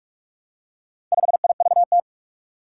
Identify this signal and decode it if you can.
Morse code, heard as a keyed tone, about 44 wpm, 700 Hz. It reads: 5IVT
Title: Four Challenging Letter-Number Combinations 45wpm